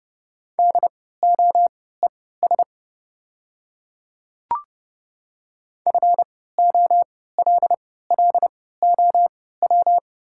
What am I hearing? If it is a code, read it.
Morse code, heard as a keyed tone, about 30 wpm, spaced as for 17 wpm, 700 Hz. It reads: DOES FOLLOW
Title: 3x Character Spacing - Top 200 Words - 30wpm